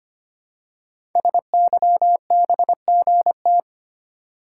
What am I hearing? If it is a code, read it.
Morse code, heard as a keyed tone, about 25 wpm, 700 Hz. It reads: SYBGT
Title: Code Groups - 5 Letters - 25wpm